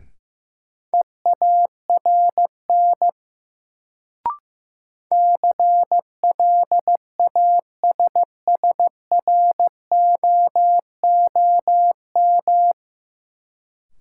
Morse code, heard as a keyed tone, about 15 wpm, 700 Hz. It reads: EARN CLASSROOM
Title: Sentences from Top 1500 Words - New Word Drill Alt - 15wpm